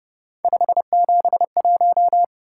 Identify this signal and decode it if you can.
Morse code, heard as a keyed tone, about 30 wpm, 700 Hz. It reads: 571